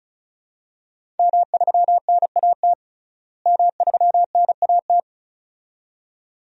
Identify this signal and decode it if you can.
Morse code, heard as a keyed tone, about 34 wpm, 700 Hz. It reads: M3NAT M3NAT
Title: DXCC-100 Call Signs - 5 Characters - Repeated 2x - 35wpm